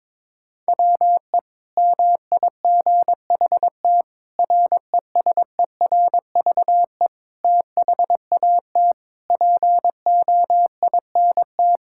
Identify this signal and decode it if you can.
Morse code, heard as a keyed tone, about 22 wpm, 700 Hz. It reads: WE MIGHT RESERVE THAT POINT
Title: Sentences from Top 1000 Words - Review Part 6 - 22wpm